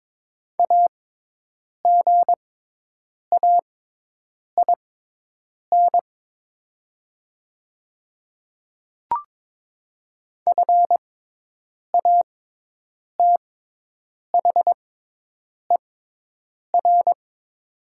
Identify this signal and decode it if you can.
Morse code, heard as a keyed tone, about 22 wpm, spaced as for 8 wpm, 700 Hz. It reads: AGAIN FATHER
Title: 6x Character Spacing - Top 200 Words - 22wpm